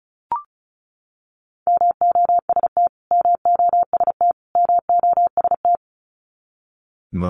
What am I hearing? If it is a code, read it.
Morse code, heard as a keyed tone, about 34 wpm, 700 Hz. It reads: MOST MOST MOST
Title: Top 200 Words Repeated 35wpm